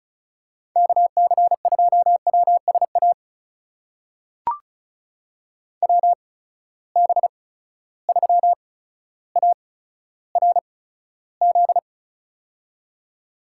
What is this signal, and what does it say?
Morse code, heard as a keyed tone, about 36 wpm, spaced as for 15 wpm, 700 Hz. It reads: KC2WSA WB3ARZ
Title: USA Call Signs - 6 Characters - 8x Character Spacing - 35wpm